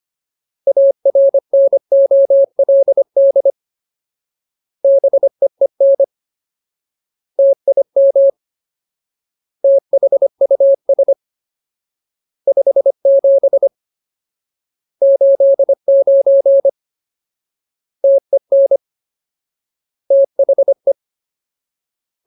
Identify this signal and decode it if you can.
Morse code, heard as a keyed tone, about 25 wpm, 550 Hz. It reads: ARNOLD BEEN TIM THUS 57 89 TEN THE
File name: Ultimate Word List / 25WPM Ultimate